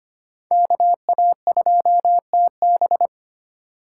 Morse code, heard as a keyed tone, about 25 wpm, 700 Hz. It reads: KA2TB
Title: Code Groups - 5 Letter:Numbers - 25wpm